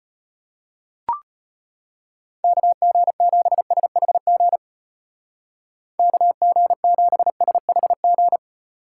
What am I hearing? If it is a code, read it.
Morse code, heard as a keyed tone, about 30 wpm, 700 Hz. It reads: KG7SHG KG7SHG
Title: Speed Racing - USA 2x3 Call Signs 25wpm